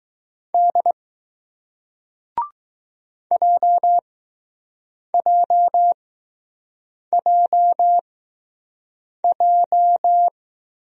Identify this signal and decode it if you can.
Morse code, heard as a keyed tone, about 20 wpm, spaced as for 6 wpm, 700 Hz. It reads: D JJJJ